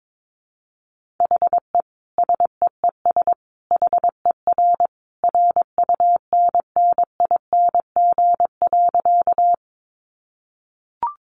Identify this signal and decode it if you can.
Morse code, heard as a keyed tone, about 22 wpm, 700 Hz. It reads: HE SEES HER RUNNING.
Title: Additional Easy Phrases 22wpm